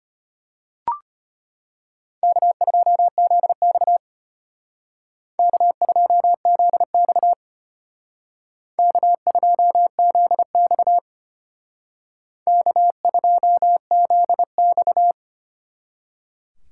Morse code, heard as a keyed tone, about 33 wpm, 700 Hz. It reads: K2ZX K2ZX K2ZX K2ZX